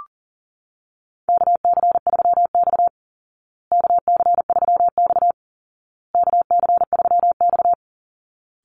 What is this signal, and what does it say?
Morse code, heard as a keyed tone, about 40 wpm, 700 Hz. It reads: KC3X KC3X KC3X
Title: Call Signs Repeated 2x - 40wpm